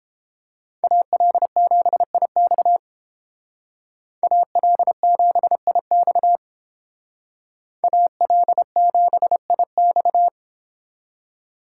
Morse code, heard as a keyed tone, about 31 wpm, 700 Hz. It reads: AL7IX AL7IX AL7IX